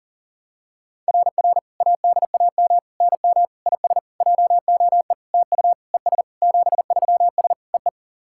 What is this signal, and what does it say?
Morse code, heard as a keyed tone, about 40 wpm, 690 Hz. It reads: RR ADAM NM IS JOE TU ES 73S EE
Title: SOTA QSO Practice 40wpm